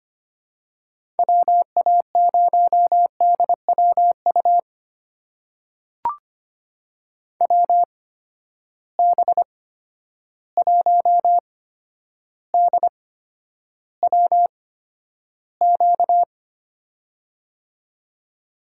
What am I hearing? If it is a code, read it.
Morse code, heard as a keyed tone, about 25 wpm, spaced as for 11 wpm, 700 Hz. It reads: WA0DWU WB1DWQ
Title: USA Call Signs - 6 Characters - 8x Character Spacing - 25wpm